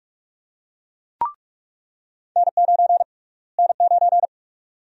Morse code, heard as a keyed tone, about 42 wpm, 700 Hz. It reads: N9 N9
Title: USA Call Sign Prefixes - Repeated 2x - 45wpm